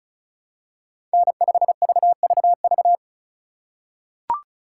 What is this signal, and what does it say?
Morse code, heard as a keyed tone, about 36 wpm, 700 Hz. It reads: N5VVV
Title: USA Call Signs - 5 Characters - 2x Character Spacing - 35wpm